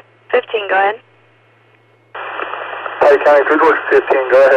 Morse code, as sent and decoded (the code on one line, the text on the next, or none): none